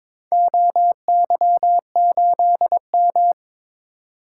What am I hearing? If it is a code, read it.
Morse code, heard as a keyed tone, about 22 wpm, 700 Hz. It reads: OY8M
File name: practice / 22wpm / Four Letter-Number Combinations - 22wpm